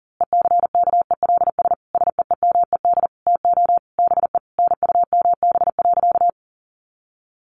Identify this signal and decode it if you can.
Morse code, heard as a keyed tone, about 40 wpm, 700 Hz. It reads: ECKELS SEEMED TO BE NUMB.
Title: A Sound Of Thunder 40wpm